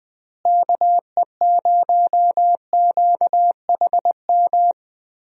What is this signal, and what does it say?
Morse code, heard as a keyed tone, about 20 wpm, 700 Hz. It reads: KE0QHM
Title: USA Call Signs - 6 Characters - Repeated 2x - 20wpm